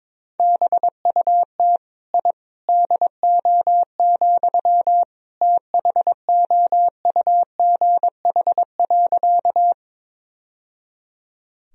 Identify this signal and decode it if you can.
Morse code, heard as a keyed tone, about 22 wpm, 700 Hz. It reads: BUT I DO, THOUGH.
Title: Sentences from Top 1000 Words - Review Part 2 - 22wpm